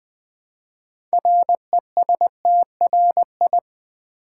Morse code, heard as a keyed tone, about 20 wpm, 700 Hz. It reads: RESTRI
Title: Build A Word - 3 Letters At A Time - 8-Letter Words - 20wpm